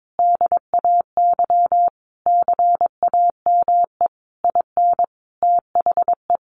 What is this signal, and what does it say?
Morse code, heard as a keyed tone, about 22 wpm, 700 Hz. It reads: DAY CAME IN THE
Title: Sentences from Top 1500 Words 22wpm